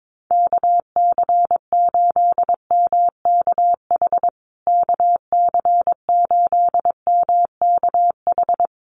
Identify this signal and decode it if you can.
Morse code, heard as a keyed tone, about 22 wpm, 700 Hz. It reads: KC8MKH KC8MKH
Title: USA Call Signs Repeated 2x 22wpm